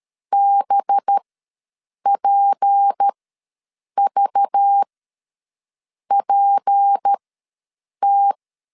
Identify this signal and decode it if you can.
Morse code, heard as a keyed tone, about 13 wpm, spaced as for 7 wpm, 800 Hz. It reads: BPVPT
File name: LEZIONI / 6 PKX0 / 6 P ws 12wpm 3117